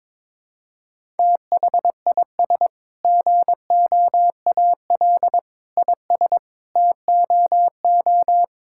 Morse code, heard as a keyed tone, about 21 wpm, 700 Hz. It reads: THIS GOAL IS TOO
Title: Sentences from Top 1000 Words - Review Part 5 - 22wpm